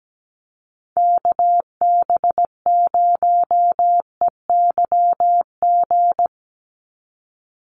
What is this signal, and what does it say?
Morse code, heard as a keyed tone, about 17 wpm, 700 Hz. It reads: KB0EYG
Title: USA Call Signs 17wpm